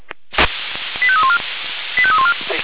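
Five-tone selective call (ZVEI), heard as quick runs of five tones, tones of about 70 ms. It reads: 85314 85314
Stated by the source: ZVEI five-tone call received by SDR.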